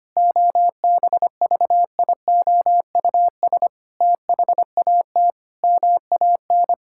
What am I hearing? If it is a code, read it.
Morse code, heard as a keyed tone, about 25 wpm, 700 Hz. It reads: OBVIOUS THAT MAN
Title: Sentences from Top 1600 Words 25wpm